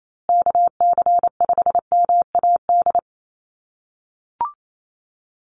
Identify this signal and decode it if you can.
Morse code, heard as a keyed tone, about 28 wpm, 700 Hz. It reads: KC5MAD
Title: USA Call Signs Repeated 3x 28wpm